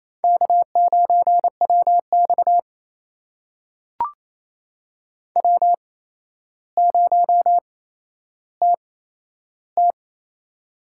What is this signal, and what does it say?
Morse code, heard as a keyed tone, about 28 wpm, spaced as for 12 wpm, 700 Hz. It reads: K9WX W0TT
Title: USA Call Signs - 4 Characters - 8x Character Spacing - 28wpm